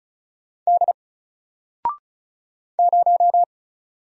Morse code, heard as a keyed tone, about 36 wpm, 700 Hz. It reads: D 0
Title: Single Letter-Number 35wpm